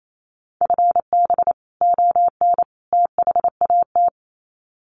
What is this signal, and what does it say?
Morse code, heard as a keyed tone, about 28 wpm, 700 Hz. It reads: FB ON THAT
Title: Common QSO Phrases 28wpm